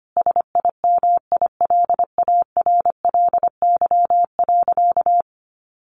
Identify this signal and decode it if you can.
Morse code, heard as a keyed tone, about 25 wpm, 700 Hz. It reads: SIMILARLY.
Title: Sentences from Top 1400 Words 25wpm